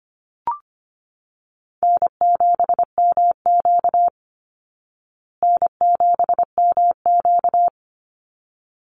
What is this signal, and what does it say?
Morse code, heard as a keyed tone, about 25 wpm, 700 Hz. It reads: N7MQ N7MQ
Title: Call Signs Repeated 3x - 25wpm